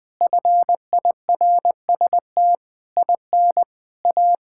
Morse code, heard as a keyed tone, about 21 wpm, 700 Hz. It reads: FIRST IN A
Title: Sets of 8 Words - 20wpm